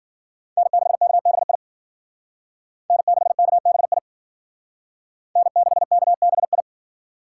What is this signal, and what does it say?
Morse code, heard as a keyed tone, about 50 wpm, 700 Hz. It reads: N6KBI N6KBI N6KBI
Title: Speed Racing - USA 1x3 Call Signs 40wpm